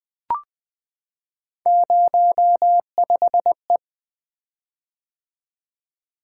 Morse code, heard as a keyed tone, about 20 wpm, 700 Hz. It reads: 05E
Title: Three Letter-Number Combinations - 20wpm